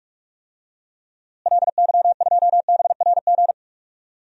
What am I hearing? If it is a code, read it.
Morse code, heard as a keyed tone, about 41 wpm, 700 Hz. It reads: LY2BRG
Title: DXCC-100 Call Signs - 6 Characters - Repeated 3x - 45wpm